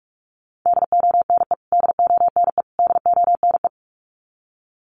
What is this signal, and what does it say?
Morse code, heard as a keyed tone, about 44 wpm, 700 Hz. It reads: DONE DONE DONE
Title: Top 500 Words Repeated 45wpm